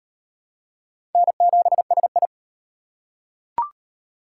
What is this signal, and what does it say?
Morse code, heard as a keyed tone, about 34 wpm, 700 Hz. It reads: N7SI